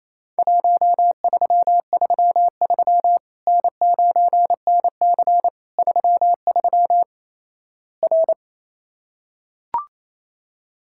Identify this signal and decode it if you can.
Morse code, heard as a keyed tone, about 28 wpm, 700 Hz. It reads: 1333 N9NC 33 R
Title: Worked All Europe DX Contest - QTC Exchange - 28wpm